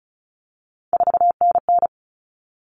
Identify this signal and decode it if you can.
Morse code, heard as a keyed tone, about 34 wpm, 700 Hz. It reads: 4NN
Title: All Cut Numbers - 3-Digit 35wpm